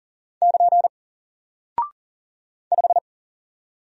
Morse code, heard as a keyed tone, about 40 wpm, 700 Hz. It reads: <KN> 5
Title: Single Character (Letter:Number:Punctuation:Prosign) 40wpm